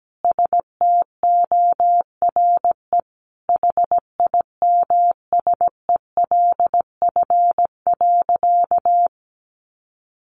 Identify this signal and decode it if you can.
Morse code, heard as a keyed tone, about 17 wpm, 700 Hz. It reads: STORE HIMSELF.
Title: Additional Easy Phrases 17wpm